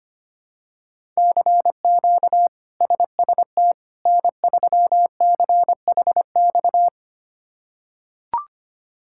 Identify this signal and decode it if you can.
Morse code, heard as a keyed tone, about 25 wpm, 700 Hz. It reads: CQ SST N3CHX
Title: K1USN SST QSO Practice - 25wpm